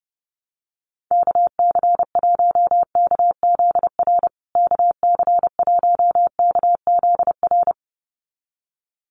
Morse code, heard as a keyed tone, about 30 wpm, 700 Hz. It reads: KC1KZR KC1KZR
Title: USA Call Signs Repeated 2x 30wpm